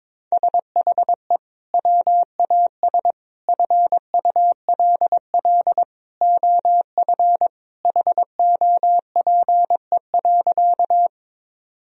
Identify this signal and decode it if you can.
Morse code, heard as a keyed tone, about 22 wpm, 700 Hz. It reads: SHE WAS FULL OF HOPE.
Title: Sentences from Top 1000 Words - Review Part 3 - 22wpm